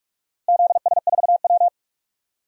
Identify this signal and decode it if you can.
Morse code, heard as a keyed tone, about 44 wpm, 700 Hz. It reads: ZS4W
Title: DXCC-100 Call Signs - 4 Characters - Repeated 3x - 45wpm